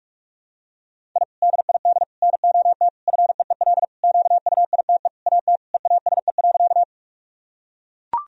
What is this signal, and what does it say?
Morse code, heard as a keyed tone, about 44 wpm, 700 Hz. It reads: I DID NOT FEEL QUITE AT EASE.
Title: Sentences from Top 1000 Words - Review Part 1 - 45wpm